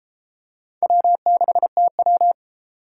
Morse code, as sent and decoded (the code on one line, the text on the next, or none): .-- -.... - .--
W6TW